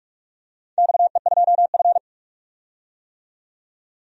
Morse code, heard as a keyed tone, about 44 wpm, 690 Hz. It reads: XE2F
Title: DXCC-100 Call Signs - 4 Characters - 45wpm